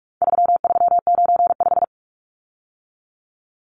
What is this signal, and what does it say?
Morse code, heard as a keyed tone, about 44 wpm, 700 Hz. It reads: 3395
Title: Numbers - 4-Digit 45wpm